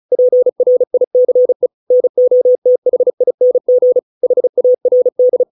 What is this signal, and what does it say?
Morse code, heard as a keyed tone, about 34 wpm, 500 Hz. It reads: PRICE NOTHING HARD